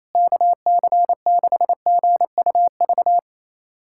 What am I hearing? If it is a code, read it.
Morse code, heard as a keyed tone, about 28 wpm, 700 Hz. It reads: KC6GUV
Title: USA Callsign DE Callsign 28wpm